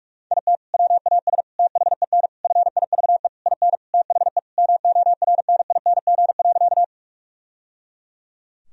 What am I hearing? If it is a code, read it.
Morse code, heard as a keyed tone, about 46 wpm, 700 Hz. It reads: IT WAS THEN FIVE IN THE MORNING.
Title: Sentences from Top 1000 Words - Review Part 2 - 45wpm